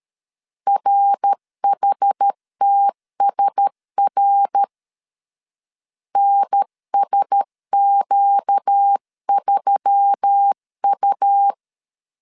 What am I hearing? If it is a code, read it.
Morse code, heard as a keyed tone, about 13 wpm, 800 Hz. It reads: RHTSR NSQ3U
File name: CW / DET / 1 6 1500C S C 12wpm 3533